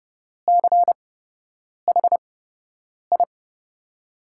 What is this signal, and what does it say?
Morse code, heard as a keyed tone, about 30 wpm, spaced as for 8 wpm, 700 Hz. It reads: CHI
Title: USA Call Signs - 6 Characters - 8x Character Spacing - 30wpm